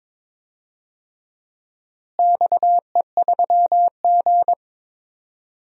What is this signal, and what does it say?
Morse code, heard as a keyed tone, about 22 wpm, 700 Hz. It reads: XE3G
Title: DXCC-100 Call Signs - 4 Characters - Repeated 2x - 22wpm